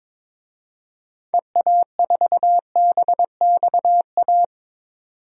Morse code, heard as a keyed tone, about 22 wpm, 700 Hz. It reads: EA4BXA
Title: DXCC-100 Call Signs - 6 Characters - 22wpm